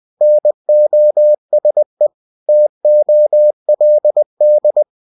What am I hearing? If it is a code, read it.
Morse code, heard as a keyed tone, about 20 wpm, 600 Hz. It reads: NOSE TOLD
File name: cw-books / winnie-the-pooh / winnie-20wpm-page-0040